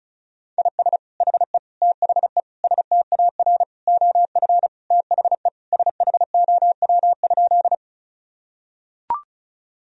Morse code, heard as a keyed tone, about 36 wpm, 700 Hz. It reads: IS HE THE STAR OF THE SHOW?
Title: Sentences from Top 1000 Words - Review Part 2 - 35wpm